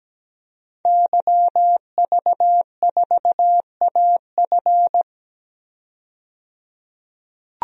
Morse code, heard as a keyed tone, about 17 wpm, 700 Hz. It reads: YV4AF
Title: DXCC-100 Call Signs - 5 Characters - 17wpm